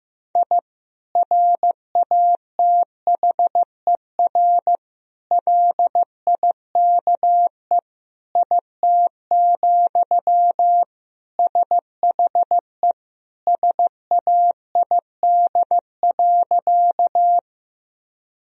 Morse code, heard as a keyed tone, about 15 wpm, 700 Hz. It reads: I RATHER LIKE IT, SHE SAID.
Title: Sentences from Top 1000 Words - Review Part 4 - 15wpm